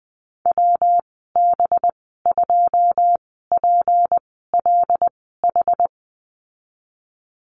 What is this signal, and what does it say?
Morse code, heard as a keyed tone, about 19 wpm, spaced as for 14 wpm, 700 Hz. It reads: WB2PLH
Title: 238 Lesson 38g - Letter q - Call Signs 20wpm